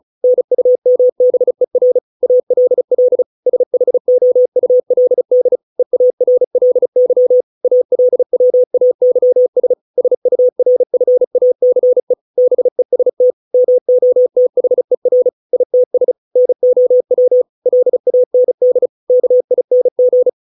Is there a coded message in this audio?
Yes: Morse code from 0.2 to 20.3 s, heard as a keyed tone, about 35 wpm, 500 Hz. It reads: NUMBER ALL SHOULD EARLY ALWAYS SURFACE BEST MOTHER ITS NOW LAND KING